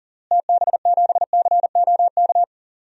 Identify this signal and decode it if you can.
Morse code, heard as a keyed tone, about 38 wpm, 700 Hz. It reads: TB7COK
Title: DXCC-100 Call Signs - 6 Characters - Repeated 3x - 40wpm